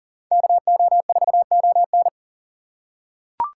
Morse code, heard as a keyed tone, about 40 wpm, 700 Hz. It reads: KO4ON
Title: USA Call Signs - 5 Characters - 2x Character Spacing - 40wpm